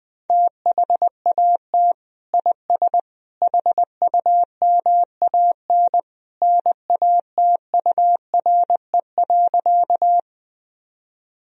Morse code, heard as a keyed tone, about 20 wpm, 700 Hz. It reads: THAT IS HUMAN NATURE.
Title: Sentences from Top 1000 Words - Review Part 3 - 20wpm